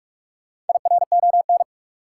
Morse code, heard as a keyed tone, about 46 wpm, 700 Hz. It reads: IRON
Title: Four Letter Words - 45wpm